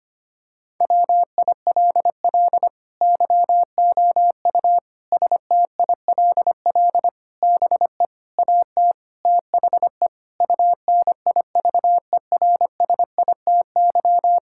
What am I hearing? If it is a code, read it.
Morse code, heard as a keyed tone, about 25 wpm, 700 Hz. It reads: WILL YOU STILL BE AT THE UNIVERSITY
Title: Sentences from Top 1000 Words - Review Part 5 - 25wpm